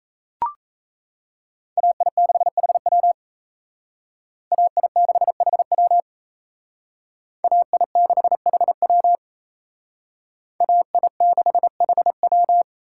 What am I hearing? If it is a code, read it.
Morse code, heard as a keyed tone, about 36 wpm, 700 Hz. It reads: AI6HW AI6HW AI6HW AI6HW